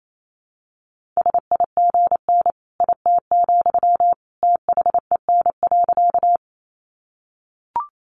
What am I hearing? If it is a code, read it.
Morse code, heard as a keyed tone, about 28 wpm, 700 Hz. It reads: SIGN IT, THEN.
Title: Sentences from Top 1000 Words 28wpm